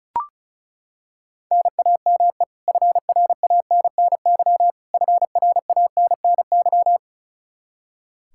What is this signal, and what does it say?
Morse code, heard as a keyed tone, about 34 wpm, 700 Hz. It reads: NAME FRANNY FRANNY
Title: Practice QSO - Exchange 1 - Beginning - 35wpm